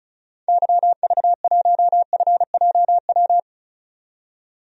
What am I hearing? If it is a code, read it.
Morse code, heard as a keyed tone, about 36 wpm, 700 Hz. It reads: YV1FJW
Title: DXCC-100 Call Signs - 6 Characters - 35wpm